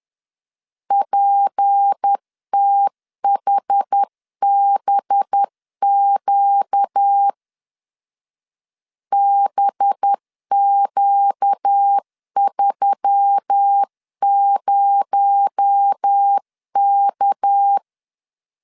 Morse code, heard as a keyed tone, about 10 wpm, 800 Hz. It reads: PTHBQ BQ30K
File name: CW / DET / 1 6 1500C S C 10wpm 4252